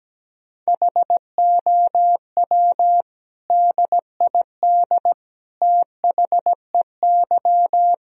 Morse code, heard as a keyed tone, about 17 wpm, 700 Hz. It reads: HOW DID THEY
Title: Sentences from Top 1000 Words - Review Part 1 - 17wpm